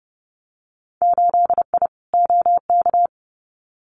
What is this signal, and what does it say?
Morse code, heard as a keyed tone, about 30 wpm, 700 Hz. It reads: 8I OK
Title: Winter Field Day Exchanges Encore 30wpm